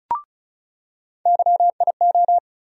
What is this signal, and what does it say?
Morse code, heard as a keyed tone, about 34 wpm, 700 Hz. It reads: YIO
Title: USA Call Sign Postfixes - 3 Letters 35wpm